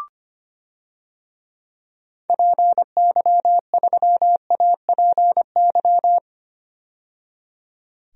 Morse code, heard as a keyed tone, about 25 wpm, 700 Hz. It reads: PY3APY
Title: DXCC-100 Call Signs - 6 Characters - 25wpm